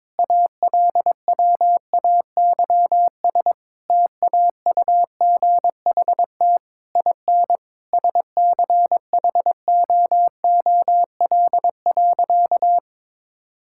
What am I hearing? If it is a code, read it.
Morse code, heard as a keyed tone, about 21 wpm, 700 Hz. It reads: ALWAYS TAUGHT IN SCHOOL.
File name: Difficult Sentences with Hints - 22wpm